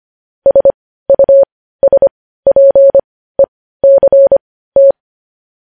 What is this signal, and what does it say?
Morse code, heard as a keyed tone, about 25 wpm, spaced as for 15 wpm, 550 Hz. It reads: SUSPECT